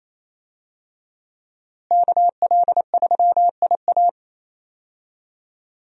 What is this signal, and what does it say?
Morse code, heard as a keyed tone, about 28 wpm, 700 Hz. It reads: KL3IA
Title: USA Call Signs - 5 Characters - Repeated 3x - 28wpm